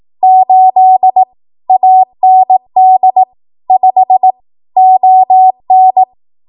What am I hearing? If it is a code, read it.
Morse code, heard as a keyed tone, about 18 wpm, 750 Hz. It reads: 8 AND 5 ON
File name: Archive / 18 WPM / 18WPM